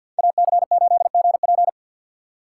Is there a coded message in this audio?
Yes: Morse code from 0.2 to 1.7 s, heard as a keyed tone, about 64 wpm, 700 Hz. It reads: AC8GP